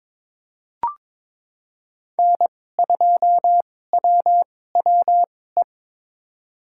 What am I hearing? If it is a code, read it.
Morse code, heard as a keyed tone, about 22 wpm, spaced as for 16 wpm, 700 Hz. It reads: N2WWE
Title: USA Call Signs - 5 Characters - 2x Character Spacing - 22wpm